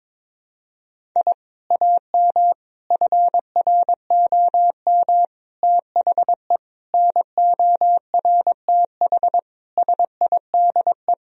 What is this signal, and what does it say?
Morse code, heard as a keyed tone, about 22 wpm, 700 Hz. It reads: I AM FROM THE NORTH SIDE
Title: Sentences from Top 1000 Words - Review Part 1 - 22wpm